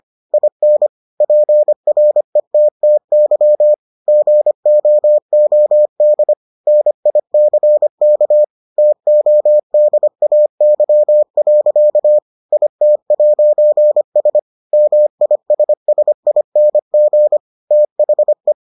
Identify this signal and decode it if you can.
Morse code, heard as a keyed tone, about 25 wpm, 600 Hz. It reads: IN PRETTY GOOD NICK TODAY. IT'S MISSING THE